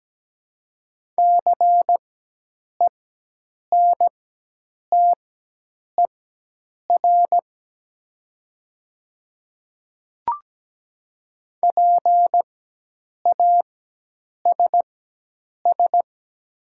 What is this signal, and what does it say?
Morse code, heard as a keyed tone, about 17 wpm, spaced as for 8 wpm, 700 Hz. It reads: CENTER PASS
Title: 4x Character Spacing - Top 500 Words - 17wpm